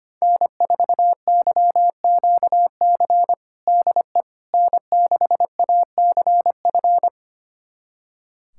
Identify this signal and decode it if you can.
Morse code, heard as a keyed tone, about 25 wpm, 700 Hz. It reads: N4YQC DE N6ACF
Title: USA Callsign DE Callsign - 5 Characters - 25wpm